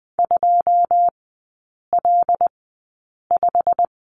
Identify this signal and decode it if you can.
Morse code, heard as a keyed tone, about 20 wpm, spaced as for 8 wpm, 700 Hz. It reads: 2L5
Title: 100 Lesson 17e - Letter u - Three Characters 20wpm